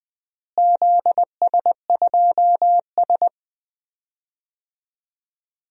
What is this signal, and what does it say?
Morse code, heard as a keyed tone, about 20 wpm, 700 Hz. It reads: ZS2S
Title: DXCC-100 Call Signs - 4 Characters - Repeated 2x - 20wpm